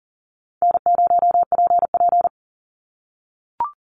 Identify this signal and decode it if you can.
Morse code, heard as a keyed tone, about 40 wpm, 700 Hz. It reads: N0PP
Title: Easy Call Signs - 40wpm